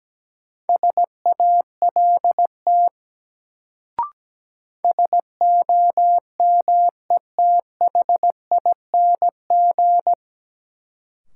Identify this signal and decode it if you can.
Morse code, heard as a keyed tone, about 17 wpm, 700 Hz. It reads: SALT SOMETHING
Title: Sets of 5 Words - New Word Drill - 17wpm